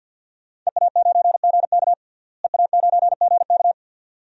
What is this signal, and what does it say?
Morse code, heard as a keyed tone, about 53 wpm, 700 Hz. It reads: EA9GK EA9GK
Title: DXCC-100 Call Signs - 5 Characters - Repeated 2x - 50wpm